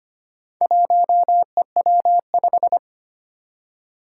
Morse code, heard as a keyed tone, about 25 wpm, 700 Hz. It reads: 1EW5
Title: Four Challenging Letter-Number Combinations 25wpm